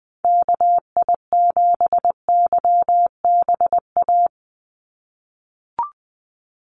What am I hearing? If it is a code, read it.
Morse code, heard as a keyed tone, about 20 wpm, 700 Hz. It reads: KI7YBA
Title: USA Call Signs 20wpm